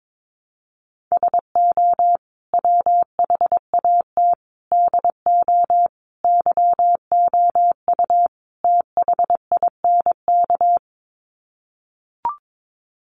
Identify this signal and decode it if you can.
Morse code, heard as a keyed tone, about 22 wpm, 700 Hz. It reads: SO WHAT DO YOU THINK
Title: Sets of 5 Words Encore2 22wpm